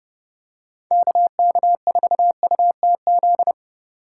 Morse code, heard as a keyed tone, about 31 wpm, 700 Hz. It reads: KK4UTZ